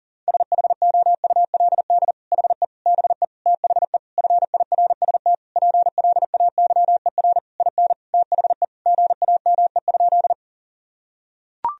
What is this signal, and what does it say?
Morse code, heard as a keyed tone, about 37 wpm, 700 Hz. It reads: SHOULD HE BE THE FIRST PLAYER IN THE GAME?
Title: Sentences from Top 1000 Words - Review Part 4 - 40wpm